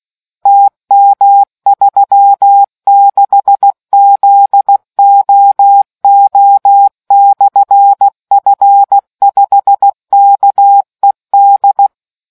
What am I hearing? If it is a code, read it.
Morse code, heard as a keyed tone, about 16 wpm, 800 Hz. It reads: TM36ZOO/F5KED